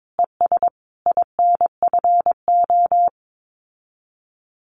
Morse code, heard as a keyed tone, about 22 wpm, 700 Hz. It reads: ES INFO
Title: Common QSO Phrases Encore 22wpm